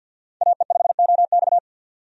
Morse code, heard as a keyed tone, about 48 wpm, 700 Hz. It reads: AE5OX
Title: USA Call Signs - 5 Characters - 50wpm